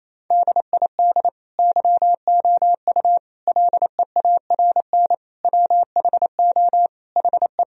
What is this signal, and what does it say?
Morse code, heard as a keyed tone, about 28 wpm, 700 Hz. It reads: DID YOU LEARN WHO HE
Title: Sentences from Top 1000 Words - Review Part 1 - 28wpm